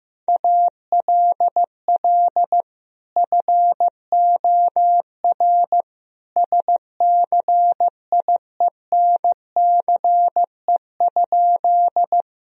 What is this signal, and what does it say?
Morse code, heard as a keyed tone, about 15 wpm, 700 Hz. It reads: ALL FOR SCIENCE?
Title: Sentences from Top 1000 Words - Review Part 2 - 15wpm